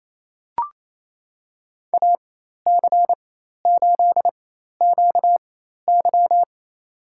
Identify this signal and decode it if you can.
Morse code, heard as a keyed tone, about 27 wpm, spaced as for 13 wpm, 700 Hz. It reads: AC8QY
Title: USA Call Signs - 5 Characters - 4x Character Spacing - 28wpm